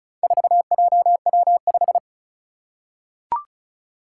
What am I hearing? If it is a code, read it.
Morse code, heard as a keyed tone, about 36 wpm, 700 Hz. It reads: 4JW5
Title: Four Challenging Letter-Number Combinations 35wpm